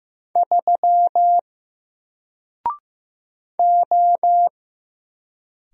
Morse code, heard as a keyed tone, about 15 wpm, 700 Hz. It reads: 3 O
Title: Single Character (Letter:Number:Punctuation:Prosign) Alt 15wpm